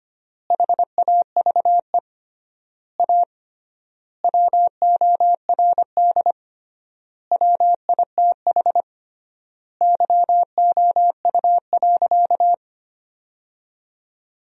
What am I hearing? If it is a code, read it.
Morse code, heard as a keyed tone, about 25 wpm, 700 Hz. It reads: HAVE A WORD WITH YOU.
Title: 3x Word-Spacing - Sentences from Top 300 Words - 25wpm